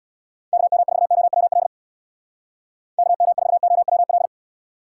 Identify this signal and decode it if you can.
Morse code, heard as a keyed tone, about 62 wpm, 700 Hz. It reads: KM4OWL KM4OWL
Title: Speed Racing - USA Call Signs 50wpm